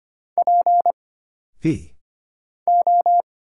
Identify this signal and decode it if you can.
Morse code, heard as a keyed tone, about 25 wpm, 700 Hz. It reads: P O
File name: Single Character (Letter:Number:Punctuation:Prosign) - Mind-Melt - 25wpm